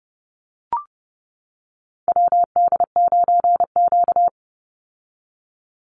Morse code, heard as a keyed tone, about 30 wpm, 700 Hz. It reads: WD9Q
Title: Call Signs - 30wpm